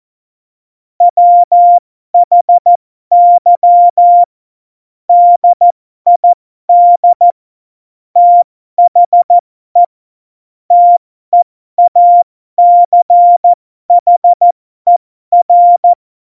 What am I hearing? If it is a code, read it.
Morse code, heard as a keyed tone, about 14 wpm, spaced as for 12 wpm, 700 Hz. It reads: WHY DID THE TEACHER
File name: Morsekurs / Witze